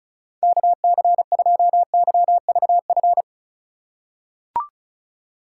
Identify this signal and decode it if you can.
Morse code, heard as a keyed tone, about 36 wpm, 700 Hz. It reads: KC2YVF
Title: USA Call Signs - 6 Characters - 4x Character Spacing - 35wpm